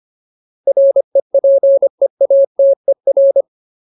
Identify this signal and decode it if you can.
Morse code, heard as a keyed tone, about 25 wpm, 550 Hz. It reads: REPEATER
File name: Ultimate Word List / 25WPM Ultimate